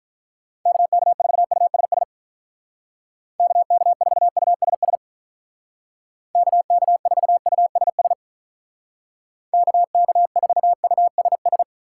Morse code, heard as a keyed tone, about 43 wpm, 700 Hz. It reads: KK4USS KK4USS KK4USS KK4USS
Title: Speed Racing - USA 2x3 Call Signs 35wpm